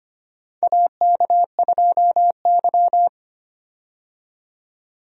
Morse code, heard as a keyed tone, about 25 wpm, 700 Hz. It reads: AK2Y
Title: USA Call Signs - 4 Characters - 25wpm